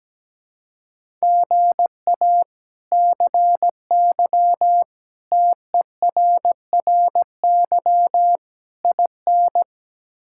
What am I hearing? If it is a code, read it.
Morse code, heard as a keyed tone, about 17 wpm, 700 Hz. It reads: GA CY TERRY IN
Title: K1USN SST QSO Practice - 17wpm